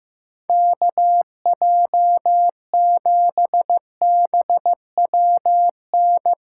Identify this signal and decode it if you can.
Morse code, heard as a keyed tone, about 17 wpm, 700 Hz. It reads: KJ7BWN
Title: Speed Racing - USA 2x3 Call Signs 15wpm